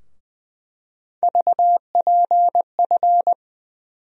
Morse code, heard as a keyed tone, about 20 wpm, 700 Hz. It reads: VPF
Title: USA Call Sign Postfixes - 3 Letters 20wpm